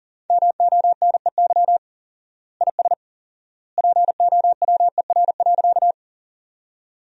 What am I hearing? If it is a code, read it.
Morse code, heard as a keyed tone, about 38 wpm, 700 Hz. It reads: MONEY IS POWER.
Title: 4x Word-Spacing - Sentences from Top 500 Words - 40wpm